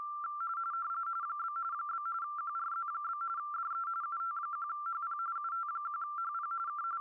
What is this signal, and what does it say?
RTTY, about 45.45 baud. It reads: ? ? QRZ? DE UT1HP UT1HP K